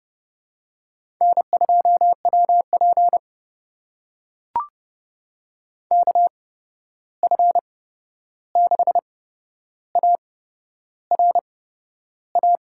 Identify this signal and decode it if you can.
Morse code, heard as a keyed tone, about 30 wpm, spaced as for 11 wpm, 700 Hz. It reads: N2WP KF6ARA
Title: USA Call Signs - 8x Character Spacing - 30wpm